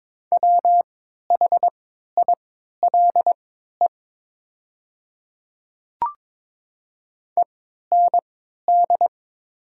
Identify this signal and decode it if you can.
Morse code, heard as a keyed tone, about 22 wpm, spaced as for 13 wpm, 700 Hz. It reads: WHILE END
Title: 3x Character Spacing - Top 300 Words - 22wpm